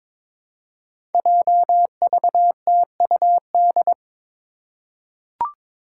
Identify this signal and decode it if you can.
Morse code, heard as a keyed tone, about 22 wpm, 700 Hz. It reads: JVTUD
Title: Five Challenging Letter-Number Combinations 22wpm